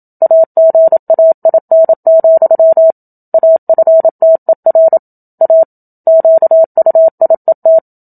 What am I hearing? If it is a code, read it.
Morse code, heard as a keyed tone, about 27 wpm, 650 Hz. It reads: AGAIN, AFTER A QUIET
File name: Book Two / 01-Chapter 1 - 27 WPM